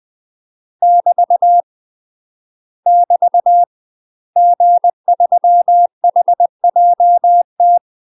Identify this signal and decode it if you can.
Morse code, heard as a keyed tone, about 20 wpm, 700 Hz. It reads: <BT> <BT> G3HJT
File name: UK MASTER QSOs TRAINING / 20WPM / UK MASTER QSOs 0024